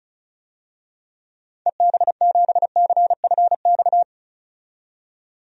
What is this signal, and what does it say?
Morse code, heard as a keyed tone, about 36 wpm, 700 Hz. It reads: EB7CFX
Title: DXCC-100 Call Signs - 6 Characters - 35wpm